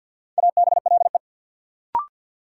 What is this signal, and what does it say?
Morse code, heard as a keyed tone, about 52 wpm, 700 Hz. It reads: ABLE
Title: Sets of 2 Words - New Word Drill - 50wpm